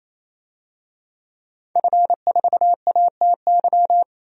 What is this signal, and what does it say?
Morse code, heard as a keyed tone, about 28 wpm, 700 Hz. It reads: F4ATY
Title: DXCC-100 Call Signs - 5 Characters - 28wpm